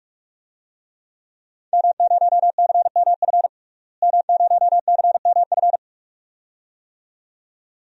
Morse code, heard as a keyed tone, about 41 wpm, 700 Hz. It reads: M0CMF M0CMF
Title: DXCC-100 Call Signs - 5 Characters - Repeated 2x - 45wpm